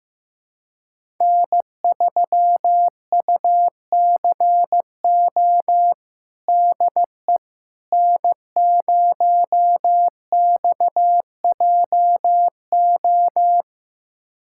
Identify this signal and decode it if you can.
Morse code, heard as a keyed tone, about 15 wpm, 700 Hz. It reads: N3UCO DE N0XJO
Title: USA Callsign DE Callsign - 5 Characters - 15wpm